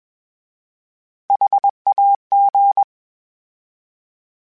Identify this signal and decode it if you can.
Morse code, heard as a keyed tone, about 21 wpm, 800 Hz. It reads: HAG